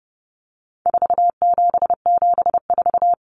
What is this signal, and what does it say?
Morse code, heard as a keyed tone, about 30 wpm, 700 Hz. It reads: 4774
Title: Common Cut Numbers - 4-Digit 30wpm